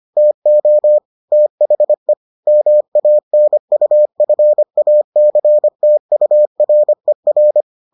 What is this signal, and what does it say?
Morse code, heard as a keyed tone, about 25 wpm, 600 Hz. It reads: TO THE MANUFACTURER